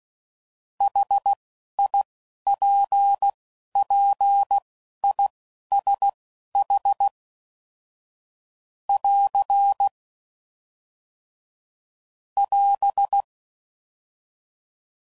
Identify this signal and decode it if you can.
Morse code, heard as a keyed tone, about 16 wpm, spaced as for 11 wpm, 800 Hz. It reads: HIPPISH <AR> <AS>
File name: Suffixes / General / ISHs